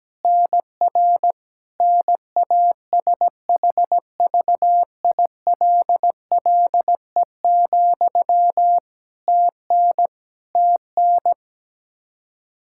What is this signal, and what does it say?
Morse code, heard as a keyed tone, about 17 wpm, 700 Hz. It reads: NR NASHVILLE, TN TN
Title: Practice QSO - Exchange 1 - Beginning - 17wpm